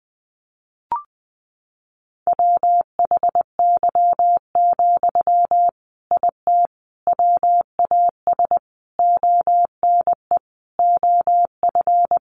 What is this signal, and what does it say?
Morse code, heard as a keyed tone, about 20 wpm, 700 Hz. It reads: WHY, IT WAS ONE OF
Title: Sentences from Top 300 Words 20wpm